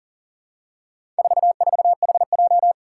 Morse code, heard as a keyed tone, about 40 wpm, 700 Hz. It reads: 44HJ
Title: Four Challenging Letter-Number Combinations 40wpm